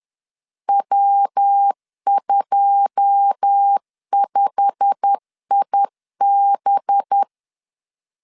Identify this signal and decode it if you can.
Morse code, heard as a keyed tone, about 10 wpm, 800 Hz. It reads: W25IB